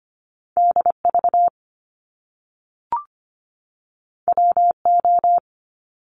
Morse code, heard as a keyed tone, about 25 wpm, 700 Hz. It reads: DV WO
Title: Top Two Letter Combinations in Words 25wpm